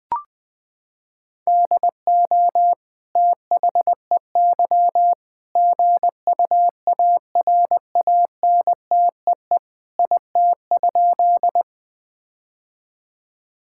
Morse code, heard as a keyed tone, about 20 wpm, 700 Hz. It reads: DO THEY GUARANTEE IT?
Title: Sentences from Top 1600 Words 20wpm